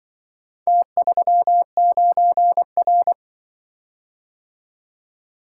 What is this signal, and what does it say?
Morse code, heard as a keyed tone, about 24 wpm, 700 Hz. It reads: T39R
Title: Speed-Racing - Four Letter-Number Combinations - 20wpm